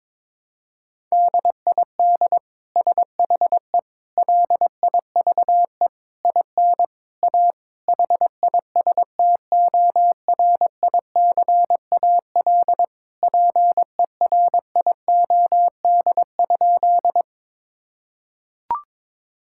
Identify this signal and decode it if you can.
Morse code, heard as a keyed tone, about 22 wpm, 700 Hz. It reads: DID SHE LIVE IN A HISTORICAL PERIOD?
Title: Sentences from Top 1000 Words - Review Part 5 - 22wpm